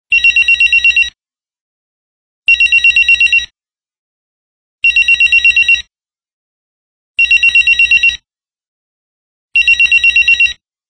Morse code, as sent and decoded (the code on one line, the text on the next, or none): none